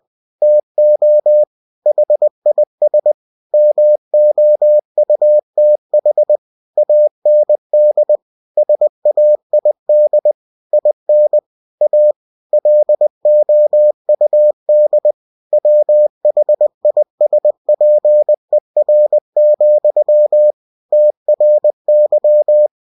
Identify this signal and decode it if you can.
Morse code, heard as a keyed tone, about 20 wpm, 600 Hz. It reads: TO HIS MOUTH AND SAID IN A LOUD WHISPER, TRY
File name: cw-books / winnie-the-pooh / winnie-20wpm-page-0107